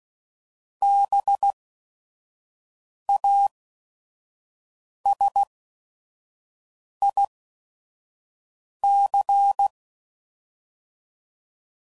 Morse code, heard as a keyed tone, about 16 wpm, spaced as for 5 wpm, 800 Hz. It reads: BASIC
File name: CW Code Groups @5WPM / Technical Subjects / SwPwrSup